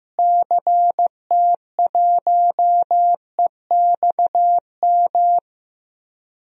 Morse code, heard as a keyed tone, about 15 wpm, 700 Hz. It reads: CT1EXM
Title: DXCC-100 Call Signs - 6 Characters - Repeated 2x - 15wpm